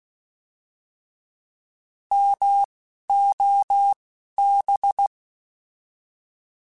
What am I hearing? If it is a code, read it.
Morse code, heard as a keyed tone, about 16 wpm, spaced as for 11 wpm, 800 Hz. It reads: MOB
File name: Alphabetical / Alphabetical mp3s / Ms @10